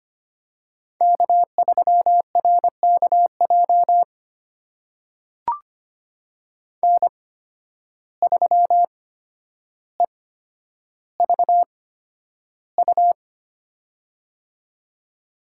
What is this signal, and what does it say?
Morse code, heard as a keyed tone, about 25 wpm, spaced as for 11 wpm, 700 Hz. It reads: K3RKJ N3EVU
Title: USA Call Signs - 5 Characters - 8x Character Spacing - 25wpm